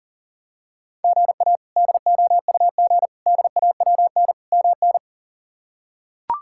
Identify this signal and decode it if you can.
Morse code, heard as a keyed tone, about 37 wpm, 700 Hz. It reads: GA DOUG DAWN MN
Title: K1USN SST QSO Practice - 40wpm